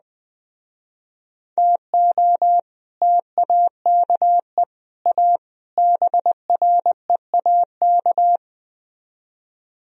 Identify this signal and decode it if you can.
Morse code, heard as a keyed tone, about 20 wpm, 700 Hz. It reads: TO TAKE A BREAK
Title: Speed-Racing - Sets of 4 Words 17wpm